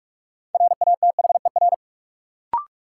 Morse code, heard as a keyed tone, about 42 wpm, 700 Hz. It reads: RATHER
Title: Sets of 2 Words - New Word Drill - 45wpm